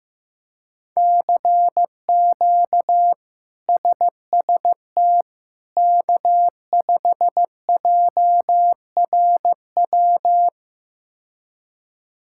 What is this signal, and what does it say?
Morse code, heard as a keyed tone, about 14 wpm, 700 Hz. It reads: CQ SST K5JRW
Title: K1USN SST QSO Practice - 15wpm